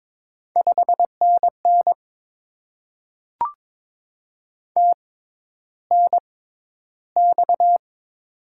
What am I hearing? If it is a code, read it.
Morse code, heard as a keyed tone, about 22 wpm, spaced as for 11 wpm, 690 Hz. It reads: 5NN TNX
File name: Top 20 QSO Elements - 6x Character Spacing - 22wpm